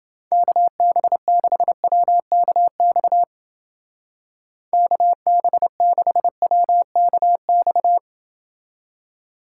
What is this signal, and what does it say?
Morse code, heard as a keyed tone, about 29 wpm, 700 Hz. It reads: KB6WKX KB6WKX